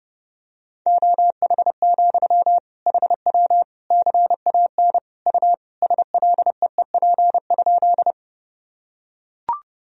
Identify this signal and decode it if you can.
Morse code, heard as a keyed tone, about 32 wpm, 700 Hz. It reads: OH, HW CAN U SLEEP?
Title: Sentences from Top 500 Words and Abbreviations 30wpm